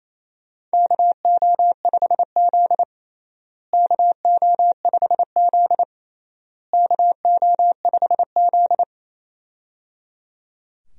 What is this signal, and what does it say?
Morse code, heard as a keyed tone, about 27 wpm, 700 Hz. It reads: KO5Z KO5Z KO5Z
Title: USA Call Signs - 4 Characters - Repeated 3x - 28wpm